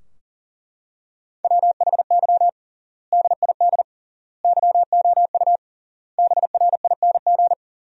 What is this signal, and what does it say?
Morse code, heard as a keyed tone, about 40 wpm, 700 Hz. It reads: WHY DID YOU BRING
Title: 3x Word-Spacing - Sentences from Top 500 Words - 40wpm